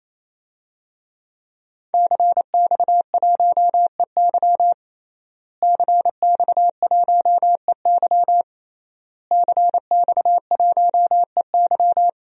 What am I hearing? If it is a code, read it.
Morse code, heard as a keyed tone, about 28 wpm, 700 Hz. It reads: CX1EY CX1EY CX1EY
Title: DXCC-100 Call Signs - 5 Characters - Repeated 3x - 28wpm